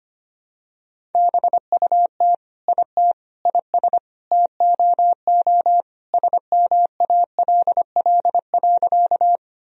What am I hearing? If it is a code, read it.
Morse code, heard as a keyed tone, about 25 wpm, 700 Hz. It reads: BUT IT IS TOO SMALL.
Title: Sentences from Top 1000 Words - Review Part 1 - 25wpm